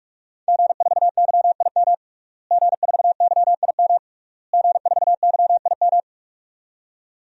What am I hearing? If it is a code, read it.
Morse code, heard as a keyed tone, about 47 wpm, 700 Hz. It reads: G4YIM G4YIM G4YIM
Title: DXCC-100 Call Signs - 5 Characters - Repeated 3x - 45wpm